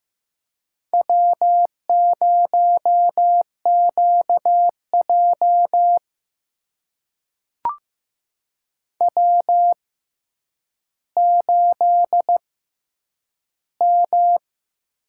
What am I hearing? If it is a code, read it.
Morse code, heard as a keyed tone, about 15 wpm, spaced as for 9 wpm, 700 Hz. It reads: W0QJ W8M
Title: USA Call Signs - 4 Characters - 6x Character Spacing - 15wpm